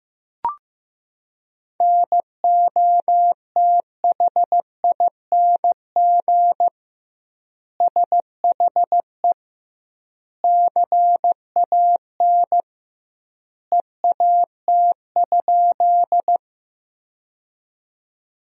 Morse code, heard as a keyed tone, about 15 wpm, 700 Hz. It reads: NOTHING SHE CAN EAT?
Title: 2x Word-Spacing - Sentences from Top 500 Words - 15wpm